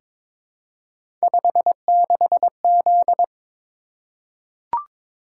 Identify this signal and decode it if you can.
Morse code, heard as a keyed tone, about 22 wpm, 700 Hz. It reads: 56Z
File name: practice / 22wpm / Three Letter-Number Combinations - 22wpm